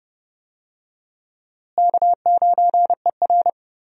Morse code, heard as a keyed tone, about 30 wpm, 700 Hz. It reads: K9ER
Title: USA Call Signs - 4 Characters - Repeated 3x - 30wpm